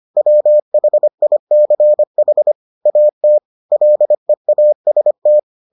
Morse code, heard as a keyed tone, about 25 wpm, 600 Hz. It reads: WHICH AT LEAST